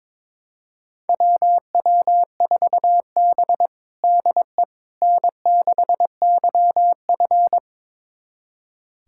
Morse code, heard as a keyed tone, about 22 wpm, 700 Hz. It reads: WW4B DE N6YF
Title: USA Callsign DE Callsign - 4 Characters - 22wpm